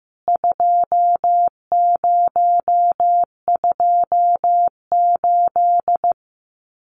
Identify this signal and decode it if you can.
Morse code, heard as a keyed tone, about 15 wpm, 700 Hz. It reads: 2028
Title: CWT Exchange 15wpm